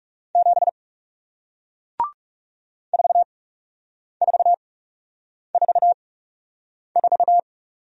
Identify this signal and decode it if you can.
Morse code, heard as a keyed tone, about 40 wpm, spaced as for 8 wpm, 700 Hz. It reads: 7 4444